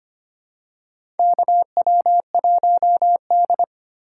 Morse code, heard as a keyed tone, about 25 wpm, 690 Hz. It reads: KW1D
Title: USA Call Signs - 4 Characters - 4x Character Spacing - 25wpm